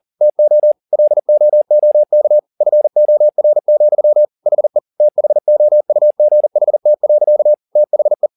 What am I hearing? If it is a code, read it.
Morse code, heard as a keyed tone, about 40 wpm, 600 Hz. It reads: TO LOOK FOR, HE THOUGHT. THE